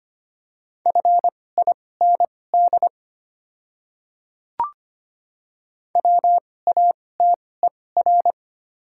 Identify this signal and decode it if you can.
Morse code, heard as a keyed tone, about 25 wpm, spaced as for 18 wpm, 700 Hz. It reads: FIND WATER
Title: 2x Character Spacing - Top 100 Words - 25wpm